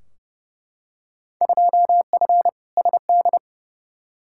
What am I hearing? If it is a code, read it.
Morse code, heard as a keyed tone, about 31 wpm, 700 Hz. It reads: 2F SD
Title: Field Day Exchanges Encore - 30wpm